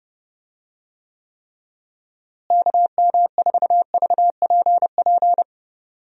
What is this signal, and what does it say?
Morse code, heard as a keyed tone, about 30 wpm, 700 Hz. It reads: KM4VPP
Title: USA Call Signs - 6 Characters - Repeated 3x - 30wpm